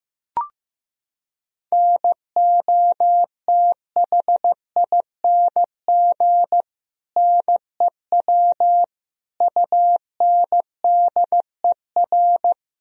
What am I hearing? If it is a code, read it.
Morse code, heard as a keyed tone, about 15 wpm, 700 Hz. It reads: NOTHING NEW UNDER
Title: Sentences from Top 1000 Words - Review Part 1 - 15wpm